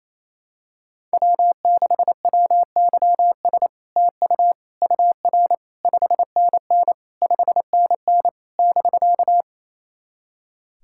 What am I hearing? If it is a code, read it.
Morse code, heard as a keyed tone, about 28 wpm, 700 Hz. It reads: W6WYS TU UR 5NN 5NN <BK>
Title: SOTA QSO Practice 28wpm